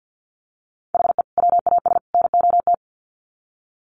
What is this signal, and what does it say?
Morse code, heard as a keyed tone, about 50 wpm, 700 Hz. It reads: HE WAS NOT
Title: Sets of 3 Words 50wpm